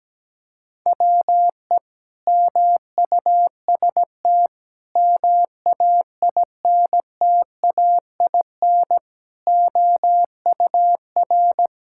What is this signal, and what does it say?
Morse code, heard as a keyed tone, about 17 wpm, 700 Hz. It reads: WE MUST MAINTAIN OUR
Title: Sentences from Top 1000 Words - Review Part 5 - 17wpm